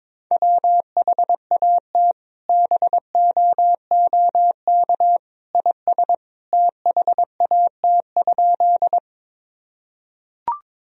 Morse code, heard as a keyed tone, about 22 wpm, 700 Hz. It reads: WHAT BOOK IS THAT?
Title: Sentences from Top 1000 Words - Review Part 1 - 22wpm